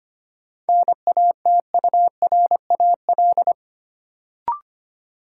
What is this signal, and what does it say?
Morse code, heard as a keyed tone, about 25 wpm, 700 Hz. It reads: NATURAL
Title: Sentences from Top 700 Words - New Word Drill - 25wpm